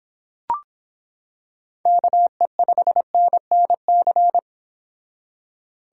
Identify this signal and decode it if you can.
Morse code, heard as a keyed tone, about 26 wpm, 700 Hz. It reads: KE5NNC